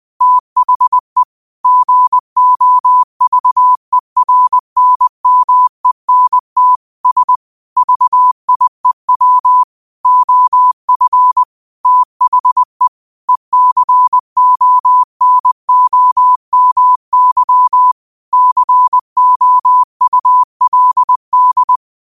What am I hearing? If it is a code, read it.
Morse code, heard as a keyed tone, about 20 wpm, 1000 Hz. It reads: THE GOVERNMENT S VIEW OF THE ECONOMY COULD
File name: qotdfiles / 20wpm